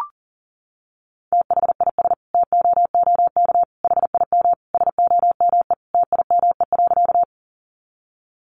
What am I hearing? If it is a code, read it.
Morse code, heard as a keyed tone, about 40 wpm, 700 Hz. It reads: THIS TOOK HIM SOME TIME.
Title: Sentences from Top 300 Words 40wpm